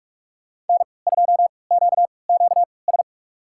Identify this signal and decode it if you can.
Morse code, heard as a keyed tone, about 46 wpm, spaced as for 25 wpm, 700 Hz. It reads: N2QQS
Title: USA Call Signs - 5 Characters - 3x Character Spacing - 45wpm